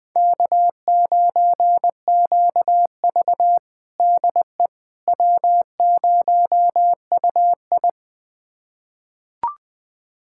Morse code, heard as a keyed tone, about 20 wpm, 700 Hz. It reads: K9QV DE W0UI
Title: USA Callsign DE Callsign - 4 Characters - 20wpm